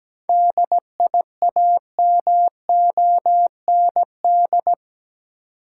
Morse code, heard as a keyed tone, about 17 wpm, 700 Hz. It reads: DIAMOND
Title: Seven Letter Words - 17wpm